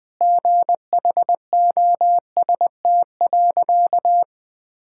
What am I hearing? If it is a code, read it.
Morse code, heard as a keyed tone, about 20 wpm, 700 Hz. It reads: GHOST.
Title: Sentences with 8-Word Phrases - Spoken Beginning - 20wpm